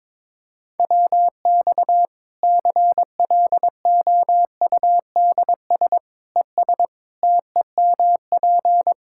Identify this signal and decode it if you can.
Morse code, heard as a keyed tone, about 21 wpm, 700 Hz. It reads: WX CLOUDS ES TEMP
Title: Practice QSO - Exchange 2 - Info - 22wpm